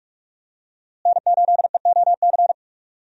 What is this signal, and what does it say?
Morse code, heard as a keyed tone, about 42 wpm, 700 Hz. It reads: N8EOC
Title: USA Call Signs - 5 Characters - Repeated 2x - 45wpm